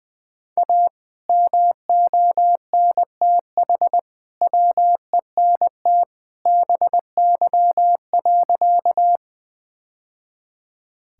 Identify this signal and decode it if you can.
Morse code, heard as a keyed tone, about 19 wpm, 700 Hz. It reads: A MONTH WENT BY.
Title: Sentences from Top 1000 Words - Review Part 5 - 20wpm